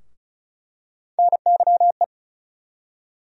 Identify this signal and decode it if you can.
Morse code, heard as a keyed tone, about 36 wpm, 700 Hz. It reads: NYE
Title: USA Call Sign Postfixes - 3 Letters 35wpm